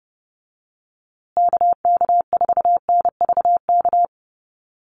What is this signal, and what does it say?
Morse code, heard as a keyed tone, about 30 wpm, 700 Hz. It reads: KK4NVK
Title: USA Call Signs Repeated 2x 30wpm